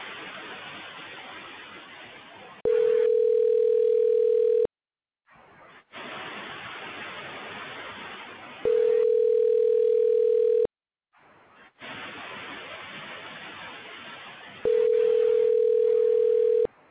Ringback tone (North America), 440 + 480 Hz.
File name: recordings / RE9d242efdd6d338a106ea8c